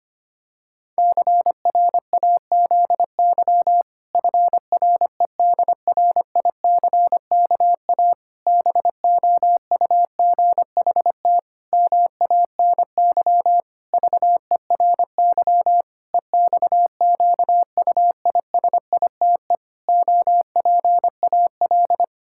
Morse code, heard as a keyed tone, about 25 wpm, 700 Hz. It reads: CRAZY FREDRICKA BOUGHT MANY VERY EXQUISITE OPAL
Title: Pangram Challenge v2 - 25wpm